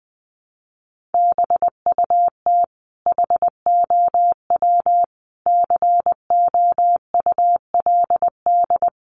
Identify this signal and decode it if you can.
Morse code, heard as a keyed tone, about 20 wpm, 700 Hz. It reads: BUT HOW COULD
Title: Sentences from Top 600 Words 20wpm